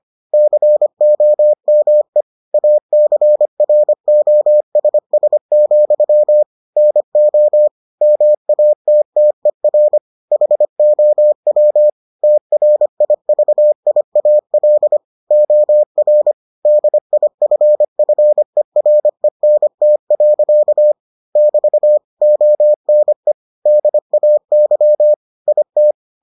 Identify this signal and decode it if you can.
Morse code, heard as a keyed tone, about 25 wpm, 600 Hz. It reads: COME ACROSS, NO MATTER HOW TRIVIAL OR DIFFERENT. <BT> ONE DAY IT